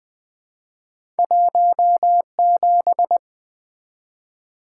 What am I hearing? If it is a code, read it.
Morse code, heard as a keyed tone, about 20 wpm, 700 Hz. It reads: 17
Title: Two Letter-Number Combinations - 20wpm